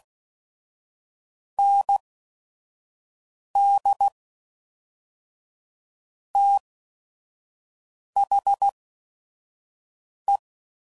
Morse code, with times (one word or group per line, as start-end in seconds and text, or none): 1.6-2.0 N
3.6-4.1 D
6.4-6.6 T
8.2-8.7 H
10.3-10.4 E